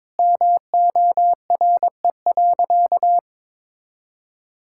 Morse code, heard as a keyed tone, about 22 wpm, 700 Hz. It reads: MORE.
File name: practice / 22wpm / Sentences with 3-Word Phrases - 22wpm